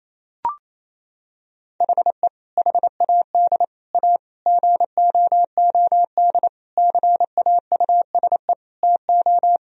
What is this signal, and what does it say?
Morse code, heard as a keyed tone, about 28 wpm, 700 Hz. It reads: HE HAD A GOOD CAUSE TO
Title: Sentences from Top 1000 Words - Review Part 1 - 28wpm